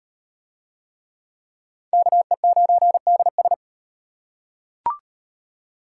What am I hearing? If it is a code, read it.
Morse code, heard as a keyed tone, about 38 wpm, 700 Hz. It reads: KE9DS